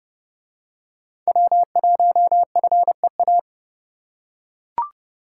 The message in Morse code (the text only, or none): W1FEA